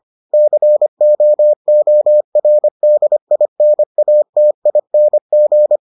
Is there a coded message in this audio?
Yes: Morse code from 0.3 to 5.8 s, heard as a keyed tone, about 25 wpm, 600 Hz. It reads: COORDINATING